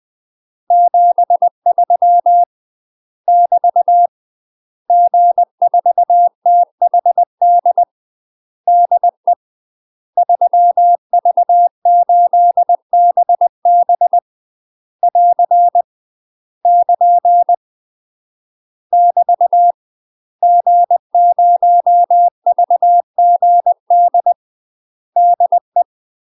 Morse code, heard as a keyed tone, about 20 wpm, 700 Hz. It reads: 73 <BT> G4THD DE 3V8BB <AR> <KN> <BT> G0VGD DE
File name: UK MASTER QSOs TRAINING / 20WPM / UK MASTER QSOs 0034